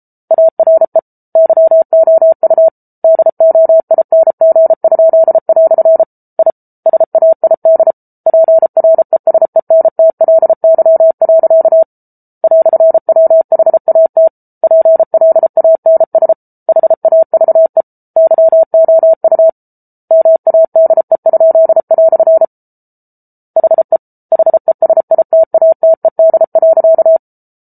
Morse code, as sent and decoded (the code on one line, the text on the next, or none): .- .-. / -.-- --- ..- / -.. --- .. -. --. ..--.. .-..-. / .. / ... .- .. -.. / .--. .-. ... -. - .-.. -.-- .-.-.- / .-..-. .-- .... .- - / .--. .-.. .- -. ... / .... .- ...- . / -.-- --- ..- / -- .- -.. ..--.. .-..-. / .... / .... ... .. - .- - . -.. .-.-.-
ARE YOU DOING?" I SAID PRESENTLY. "WHAT PLANS HAVE YOU MADE?" HE HESITATED.